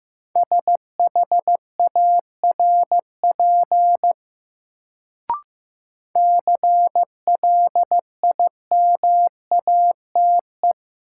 Morse code, heard as a keyed tone, about 15 wpm, 700 Hz. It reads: SHARP CLIMATE
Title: Sentences from Top 1600 Words - New Word Drill Alt - 15wpm